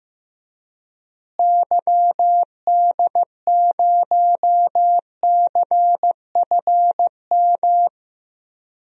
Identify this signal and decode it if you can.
Morse code, heard as a keyed tone, about 15 wpm, 700 Hz. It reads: YD0CFM
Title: DXCC-100 Call Signs - 6 Characters - Repeated 3x - 15wpm